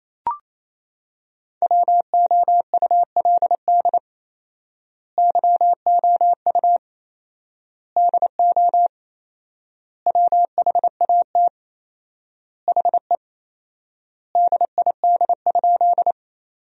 Morse code, heard as a keyed tone, about 28 wpm, 700 Hz. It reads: WOULD YOU DO WHAT HE DID?
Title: 4x Word-Spacing - Sentences from Top 100 Words - 28wpm